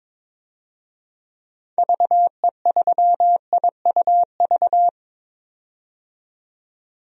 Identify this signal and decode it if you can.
Morse code, heard as a keyed tone, about 22 wpm, 700 Hz. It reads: VE3IUV
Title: DXCC-100 Call Signs - 6 Characters - 22wpm